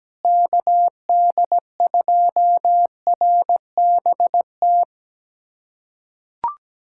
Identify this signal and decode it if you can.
Morse code, heard as a keyed tone, about 17 wpm, 700 Hz. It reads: KD2RBT
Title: USA Callsign DE Callsign - 6 Characters - 17wpm